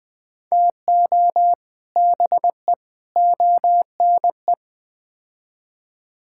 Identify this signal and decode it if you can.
Morse code, heard as a keyed tone, about 20 wpm, 700 Hz. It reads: TO BE ONE
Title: Build A Phrase - 4-Word Phrases - 20wpm